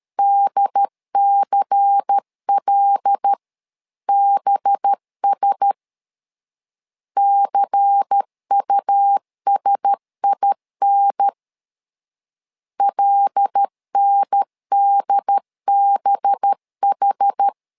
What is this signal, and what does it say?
Morse code, heard as a keyed tone, about 13 wpm, 800 Hz. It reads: DCLBS CUSIN LNDBH
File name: LEZIONI / 5 LBVC / 5 LBVC 12wpm 1955